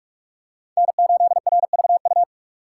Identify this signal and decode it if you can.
Morse code, heard as a keyed tone, about 38 wpm, 700 Hz. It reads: N8RVU